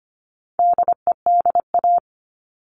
Morse code, heard as a keyed tone, about 25 wpm, 700 Hz. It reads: DEDA
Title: All Cut Numbers - 4-Digit 25wpm